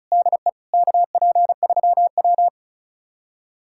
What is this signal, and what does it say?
Morse code, heard as a keyed tone, about 36 wpm, 700 Hz. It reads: DE KP3W
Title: USA Callsign DE Callsign - 4 Characters - 35wpm